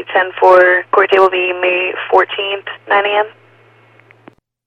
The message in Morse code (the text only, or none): none